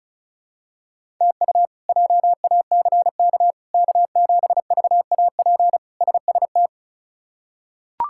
Morse code, heard as a keyed tone, about 34 wpm, 700 Hz. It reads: TU JACK K7VAP SST
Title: K1USN SST QSO Practice - 35wpm